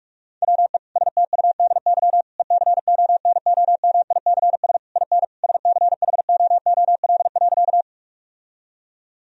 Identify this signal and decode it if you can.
Morse code, heard as a keyed tone, about 46 wpm, 700 Hz. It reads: WE STUDY ECONOMICS IN SCHOOL.
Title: Sentences from Top 1000 Words - Review Part 3 - 45wpm